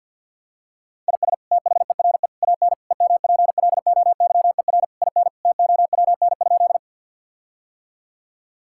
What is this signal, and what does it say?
Morse code, heard as a keyed tone, about 50 wpm, 700 Hz. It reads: IS THERE AN EMPLOYER IN TOWN?
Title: Sentences from Top 1600 Words 50wpm